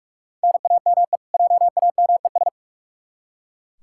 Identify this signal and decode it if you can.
Morse code, heard as a keyed tone, about 41 wpm, 700 Hz. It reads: NAME JAMES
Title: Practice QSO - Exchange 1 - Beginning - 45wpm